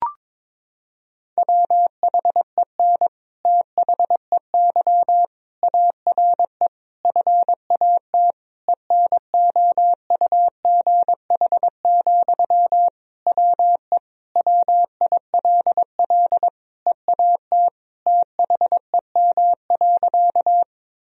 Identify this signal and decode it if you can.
Morse code, heard as a keyed tone, about 22 wpm, 700 Hz. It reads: WHEN THEY ARE FAT ENOUGH, WE WILL EAT THEM.
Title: Sentences from Top 1000 Words - Review Part 4 - 22wpm